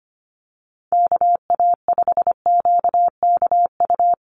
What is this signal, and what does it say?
Morse code, heard as a keyed tone, about 25 wpm, 700 Hz. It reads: KA5QKU
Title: USA Call Signs Repeated 2x 25wpm